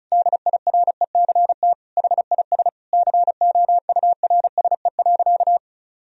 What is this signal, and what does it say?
Morse code, heard as a keyed tone, about 34 wpm, 700 Hz. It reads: DIRECT HIS COURSE.
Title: Sentences from Top 1000 Words - Review Part 2 - 35wpm